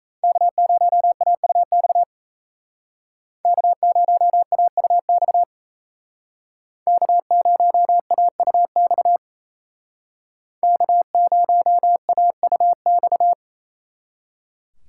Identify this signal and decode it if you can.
Morse code, heard as a keyed tone, about 32 wpm, 700 Hz. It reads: K0AUX K0AUX K0AUX K0AUX
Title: Speed Racing - USA 1x3 Call Signs 28wpm